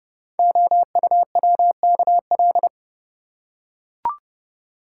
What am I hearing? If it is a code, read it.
Morse code, heard as a keyed tone, about 30 wpm, 700 Hz. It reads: OUWKL
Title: Code Groups - 5 Letters - 30wpm